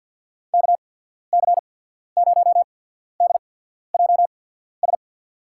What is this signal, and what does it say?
Morse code, heard as a keyed tone, about 50 wpm, spaced as for 14 wpm, 700 Hz. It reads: KC0DJS
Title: USA Call Signs - 6 Characters - 8x Character Spacing - 50wpm